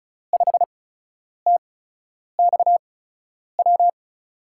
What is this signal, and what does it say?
Morse code, heard as a keyed tone, about 36 wpm, spaced as for 10 wpm, 700 Hz. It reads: 5TXW
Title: USA Call Signs - 8x Character Spacing - 35wpm